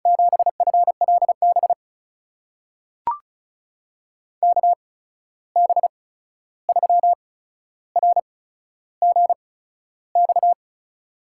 Morse code, heard as a keyed tone, about 34 wpm, spaced as for 13 wpm, 700 Hz. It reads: 7FLB KB3RGX
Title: USA Call Signs - 6 Characters - 8x Character Spacing - 35wpm